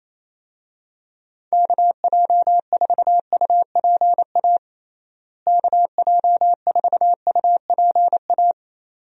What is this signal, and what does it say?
Morse code, heard as a keyed tone, about 27 wpm, 700 Hz. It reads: KJ4UPA KJ4UPA
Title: USA Call Signs - 6 Characters - Repeated 2x - 28wpm